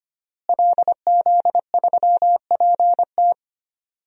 Morse code, heard as a keyed tone, about 25 wpm, 700 Hz. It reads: LZ3PT
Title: DXCC-100 Call Signs - 5 Characters - Repeated 3x - 25wpm